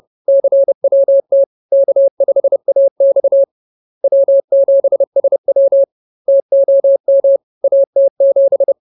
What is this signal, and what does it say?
Morse code, heard as a keyed tone, about 30 wpm, 550 Hz. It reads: CWT K5AX W7SW TOM AT7
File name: CWT-Chapter-06